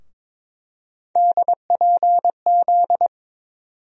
Morse code, heard as a keyed tone, about 22 wpm, 700 Hz. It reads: DPZ
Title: USA Call Sign Postfixes - 3 Letters 22wpm